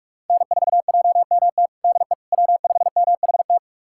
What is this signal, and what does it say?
Morse code, heard as a keyed tone, about 46 wpm, 700 Hz. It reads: N4JMT DE W5MHT
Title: USA Callsign DE Callsign - 5 Characters - 45wpm